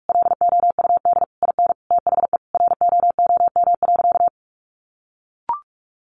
Morse code, heard as a keyed tone, about 45 wpm, 700 Hz. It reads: LOUD IN THE ROOM.
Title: A Sound Of Thunder 45wpm